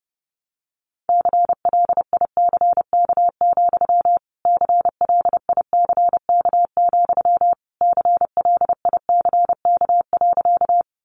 Morse code, heard as a keyed tone, about 30 wpm, 700 Hz. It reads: CLICK, CLICK, CLICK.
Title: Sentences from Top 1100 Words 30wpm